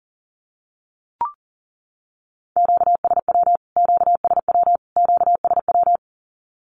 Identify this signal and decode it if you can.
Morse code, heard as a keyed tone, about 40 wpm, 700 Hz. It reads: QSW QSW QSW
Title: Q-Codes Repeated 3x 40wpm